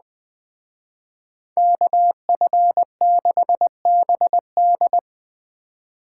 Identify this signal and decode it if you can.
Morse code, heard as a keyed tone, about 20 wpm, 700 Hz. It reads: KF6BD
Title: Speed Racing - USA 2x2 Call Signs 17wpm